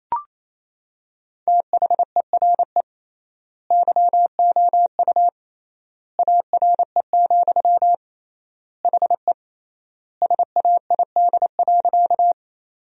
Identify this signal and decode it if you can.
Morse code, heard as a keyed tone, about 28 wpm, 700 Hz. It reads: THERE YOU ARE, HE SAID.
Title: 3x Word-Spacing - Sentences from Top 100 Words - 28wpm